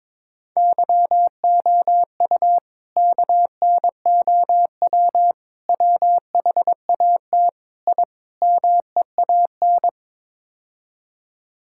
Morse code, heard as a keyed tone, about 22 wpm, 700 Hz. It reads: YOU KNOW WHAT I MEAN
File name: practice / 22wpm / Build A Phrase - 5-Word Phrases - 22wpm